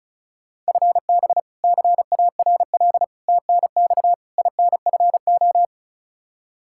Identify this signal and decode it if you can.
Morse code, heard as a keyed tone, about 36 wpm, 700 Hz. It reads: FB CARL TNX INFO
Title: Practice QSO - Exchange 3 - More Info - 35wpm